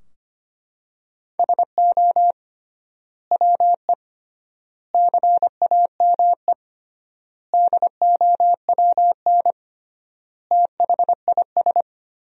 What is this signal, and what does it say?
Morse code, heard as a keyed tone, about 25 wpm, 700 Hz. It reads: SO WE CAME DOWN THIS
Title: 3x Word-Spacing - Sentences from Top 400 Words - 25wpm